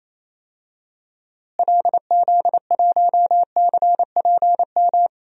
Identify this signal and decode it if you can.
Morse code, heard as a keyed tone, about 28 wpm, 700 Hz. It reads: LZ1CPM
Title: DXCC-100 Call Signs - 6 Characters - Repeated 3x - 28wpm